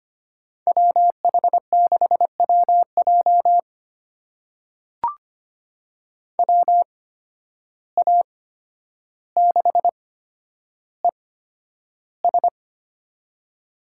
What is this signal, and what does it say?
Morse code, heard as a keyed tone, about 25 wpm, spaced as for 11 wpm, 700 Hz. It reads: WH6WJ WA6ES
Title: USA Call Signs - 5 Characters - 8x Character Spacing - 25wpm